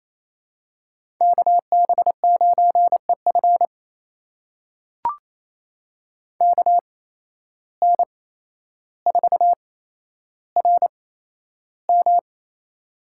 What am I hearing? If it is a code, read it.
Morse code, heard as a keyed tone, about 28 wpm, spaced as for 12 wpm, 700 Hz. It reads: KB9EF KN4RM
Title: USA Call Signs - 5 Characters - 8x Character Spacing - 28wpm